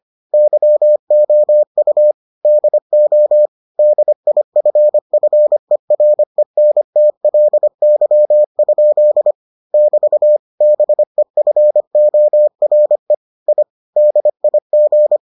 Morse code, heard as a keyed tone, about 25 wpm, 600 Hz. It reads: YOU DO DIFFERENTLY? <BT> BEFORE I DIG